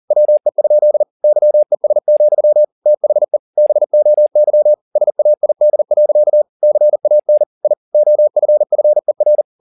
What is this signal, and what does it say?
Morse code, heard as a keyed tone, about 40 wpm, 600 Hz. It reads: WE? YES, THE BOY SAID. CAN I OFFER